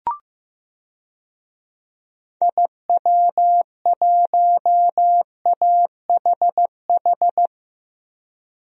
Morse code, heard as a keyed tone, about 15 wpm, 700 Hz. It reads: IW1AHH
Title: DXCC-100 Call Signs - 6 Characters - Repeated 2x - 15wpm